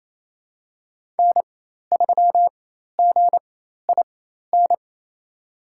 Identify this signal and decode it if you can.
Morse code, heard as a keyed tone, about 27 wpm, spaced as for 13 wpm, 700 Hz. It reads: N3GIN
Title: USA Call Signs - 5 Characters - 4x Character Spacing - 28wpm